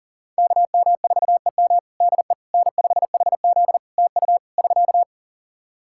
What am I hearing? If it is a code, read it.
Morse code, heard as a keyed tone, about 40 wpm, 700 Hz. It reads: KM4EM DE N5HZ TU <SK>
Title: Practice QSO - Exchange 4 - Ending - 40wpm